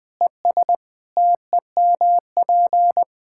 Morse code, heard as a keyed tone, about 21 wpm, 700 Hz. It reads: ES TEMP
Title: K1USN SST QSO Practice 20wpm